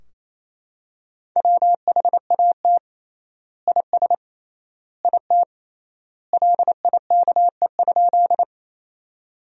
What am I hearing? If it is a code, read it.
Morse code, heard as a keyed tone, about 28 wpm, 700 Hz. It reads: WHAT IS IT LIKE?
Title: 3x Word-Spacing - Sentences from Top 100 Words - 28wpm